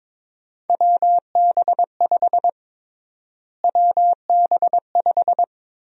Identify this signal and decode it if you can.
Morse code, heard as a keyed tone, about 21 wpm, 700 Hz. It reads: WB5 WB5
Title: USA Call Sign Prefixes - Repeated 2x - 22wpm